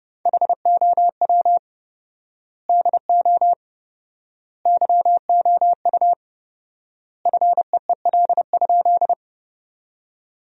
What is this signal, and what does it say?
Morse code, heard as a keyed tone, about 30 wpm, 700 Hz. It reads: HOW DO YOU FEEL?
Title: 4x Word-Spacing - Sentences from Top 400 Words - 30wpm